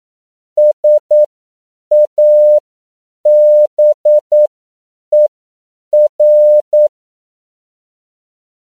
Morse code, heard as a keyed tone, about 9 wpm, spaced as for 7 wpm, 600 Hz. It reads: SABER
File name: CW / 10PPM